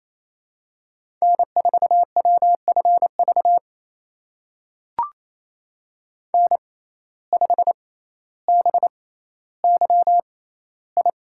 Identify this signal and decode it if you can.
Morse code, heard as a keyed tone, about 28 wpm, spaced as for 14 wpm, 700 Hz. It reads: N4WFV N5BYI
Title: USA Call Signs - 5 Characters - 6x Character Spacing - 28wpm